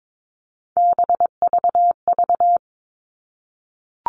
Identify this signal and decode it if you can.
Morse code, heard as a keyed tone, about 22 wpm, 700 Hz. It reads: BVV
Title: All Cut Numbers - 3-Digit 22wpm